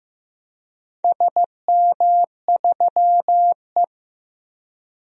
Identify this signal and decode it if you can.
Morse code, heard as a keyed tone, about 16 wpm, 700 Hz. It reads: SM3E
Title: DXCC-100 Call Signs - 4 Characters - Repeated 3x - 15wpm